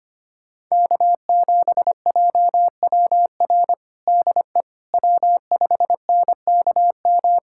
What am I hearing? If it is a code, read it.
Morse code, heard as a keyed tone, about 25 wpm, 700 Hz. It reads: K7JWR DE W5NKM
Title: USA Callsign DE Callsign 25wpm